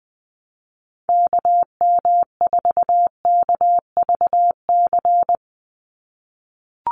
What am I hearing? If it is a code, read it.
Morse code, heard as a keyed tone, about 20 wpm, 700 Hz. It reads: KM4KVC
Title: USA Call Signs 20wpm